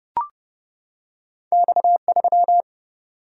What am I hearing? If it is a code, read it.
Morse code, heard as a keyed tone, about 30 wpm, 700 Hz. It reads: X3
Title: Single-Letter Single-Number 30wpm